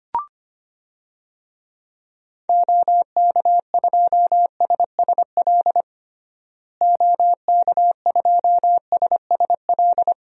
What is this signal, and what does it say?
Morse code, heard as a keyed tone, about 25 wpm, 700 Hz. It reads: OK2SSL OK2SSL
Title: DXCC-100 Call Signs - 6 Characters - Repeated 2x - 25wpm